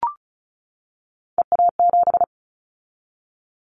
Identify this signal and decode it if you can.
Morse code, heard as a keyed tone, about 34 wpm, 700 Hz. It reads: EA7
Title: Callsign Prefixes for Top 25 DXCC - 35wpm